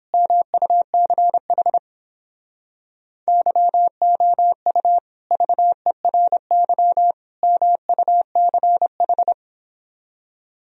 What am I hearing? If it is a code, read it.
Morse code, heard as a keyed tone, about 29 wpm, 700 Hz. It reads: MUCH YOU VERY MUCH